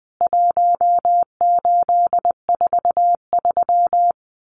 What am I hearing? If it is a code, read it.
Morse code, heard as a keyed tone, about 20 wpm, 700 Hz. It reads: 1843
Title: CWT Exchange 20wpm